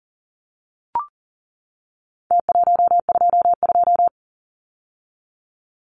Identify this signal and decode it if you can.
Morse code, heard as a keyed tone, about 40 wpm, 700 Hz. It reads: T122
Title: Numbers - Contest Style - 4-Digit 40wpm